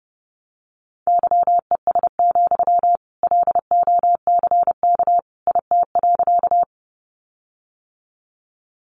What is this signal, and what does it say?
Morse code, heard as a keyed tone, about 30 wpm, 700 Hz. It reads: YES, LOCK IT.
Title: Sentences from Top 1300 Words 30wpm